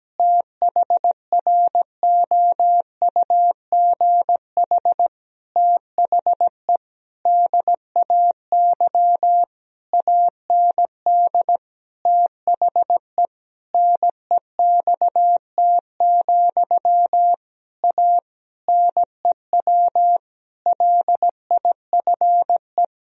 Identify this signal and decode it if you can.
Morse code, heard as a keyed tone, about 17 wpm, 700 Hz. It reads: THROUGH THE DAY AND THE NEXT, A NEW LIFE
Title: Sentences from Top 1000 Words - Review Part 1 - 17wpm